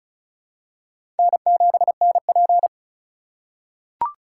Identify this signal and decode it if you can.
Morse code, heard as a keyed tone, about 34 wpm, 700 Hz. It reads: N7NP
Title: USA Call Signs - 4 Characters - 4x Character Spacing - 35wpm